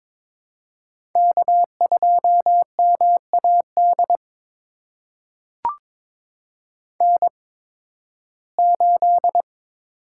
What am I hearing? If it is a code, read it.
Morse code, heard as a keyed tone, about 22 wpm, spaced as for 14 wpm, 700 Hz. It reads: K2MAD N8
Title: USA Call Signs - 5 Characters - 8x Character Spacing - 22wpm